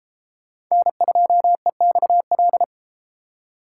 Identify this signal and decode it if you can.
Morse code, heard as a keyed tone, about 32 wpm, 700 Hz. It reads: N2EXL